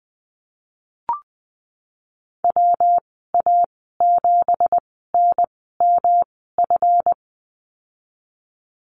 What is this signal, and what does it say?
Morse code, heard as a keyed tone, about 20 wpm, spaced as for 15 wpm, 700 Hz. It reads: WA7NMF
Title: 179 Lesson 29g - Slash - Call Signs 20wpm